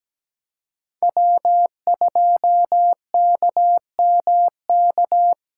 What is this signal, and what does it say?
Morse code, heard as a keyed tone, about 17 wpm, 700 Hz. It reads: W2KMK
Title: USA Callsign DE Callsign - 5 Characters - 17wpm